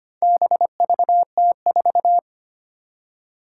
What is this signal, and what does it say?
Morse code, heard as a keyed tone, about 25 wpm, 700 Hz. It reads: BVT4
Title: Four Challenging Letter-Number Combinations 25wpm